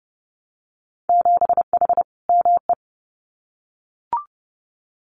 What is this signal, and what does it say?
Morse code, heard as a keyed tone, about 30 wpm, 700 Hz. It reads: 7H ME
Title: Winter Field Day Exchanges Encore 30wpm